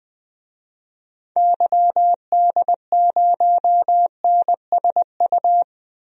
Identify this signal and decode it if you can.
Morse code, heard as a keyed tone, about 21 wpm, 700 Hz. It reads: YD0NSU
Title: DXCC-100 Call Signs - 6 Characters - Repeated 3x - 20wpm